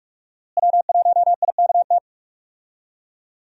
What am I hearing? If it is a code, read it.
Morse code, heard as a keyed tone, about 45 wpm, 700 Hz. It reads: W1IKT
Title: USA Call Signs - 5 Characters - Repeated 2x - 45wpm